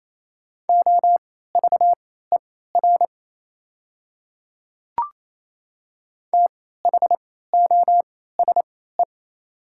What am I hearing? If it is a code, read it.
Morse code, heard as a keyed tone, about 28 wpm, spaced as for 16 wpm, 700 Hz. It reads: OVER THOSE
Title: 3x Character Spacing - Top 300 Words - 28wpm